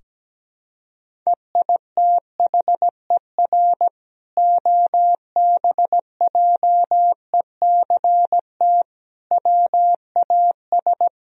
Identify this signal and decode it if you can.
Morse code, heard as a keyed tone, about 17 wpm, 700 Hz. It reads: EITHER OBJECT WAS
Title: Sentences from Top 1000 Words - Review Part 4 - 17wpm